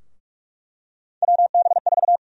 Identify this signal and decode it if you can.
Morse code, heard as a keyed tone, about 46 wpm, 700 Hz. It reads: WB4
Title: USA Call Sign Prefixes - 45wpm